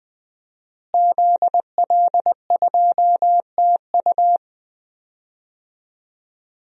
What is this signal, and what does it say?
Morse code, heard as a keyed tone, about 20 wpm, 700 Hz. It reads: ZL2TU
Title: DXCC-100 Call Signs - 5 Characters - Repeated 3x - 20wpm